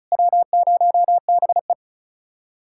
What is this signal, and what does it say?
Morse code, heard as a keyed tone, about 46 wpm, 700 Hz. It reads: W0BE